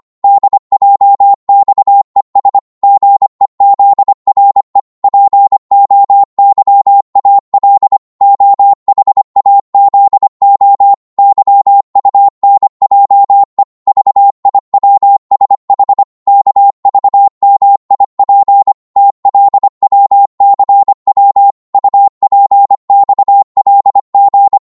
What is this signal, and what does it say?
Morse code, heard as a keyed tone, about 25 wpm, 800 Hz. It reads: DJXES GEZRE POYAL OHAZO YUNJE VIWSH KVMIP TLWCW UPXLG
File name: Morse / ltr-25wpm